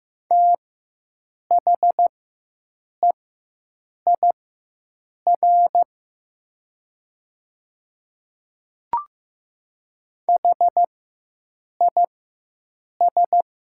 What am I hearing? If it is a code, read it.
Morse code, heard as a keyed tone, about 15 wpm, spaced as for 7 wpm, 700 Hz. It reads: THEIR HIS
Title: 4x Character Spacing - Top 100 Words - 15wpm